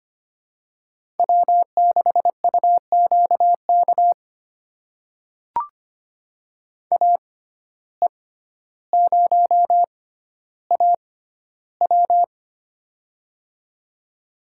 Morse code, heard as a keyed tone, about 25 wpm, spaced as for 13 wpm, 700 Hz. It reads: W6UQK AE0AW
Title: USA Call Signs - 5 Characters - 6x Character Spacing - 25wpm